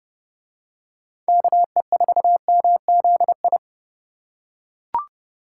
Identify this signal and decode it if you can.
Morse code, heard as a keyed tone, about 29 wpm, 700 Hz. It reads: KE4MZI